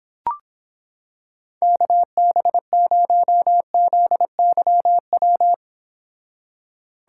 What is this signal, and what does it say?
Morse code, heard as a keyed tone, about 26 wpm, 700 Hz. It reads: KB0ZYW